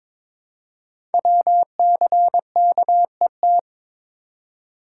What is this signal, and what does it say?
Morse code, heard as a keyed tone, about 22 wpm, 700 Hz. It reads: WCKET
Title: Code Groups - 5 Letters - 22wpm